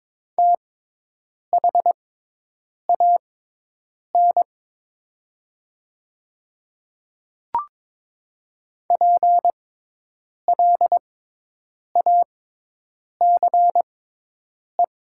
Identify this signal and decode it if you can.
Morse code, heard as a keyed tone, about 22 wpm, spaced as for 8 wpm, 700 Hz. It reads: THAN PLACE
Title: 6x Character Spacing - Top 200 Words - 22wpm